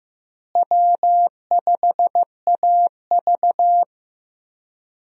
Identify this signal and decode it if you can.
Morse code, heard as a keyed tone, about 15 wpm, 700 Hz. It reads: W5AV
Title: USA Call Signs - 4 Characters - Repeated 3x - 15wpm